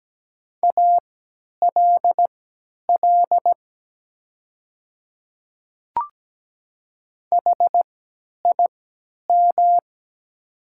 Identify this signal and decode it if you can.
Morse code, heard as a keyed tone, about 17 wpm, spaced as for 10 wpm, 700 Hz. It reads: ALL HIM
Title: 3x Character Spacing - Top 100 Words - 17wpm